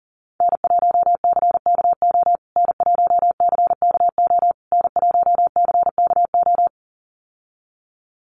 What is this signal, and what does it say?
Morse code, heard as a keyed tone, about 40 wpm, 700 Hz. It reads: N1CKO N1CKO N1CKO
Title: USA Call Signs Repeated 3x 40wpm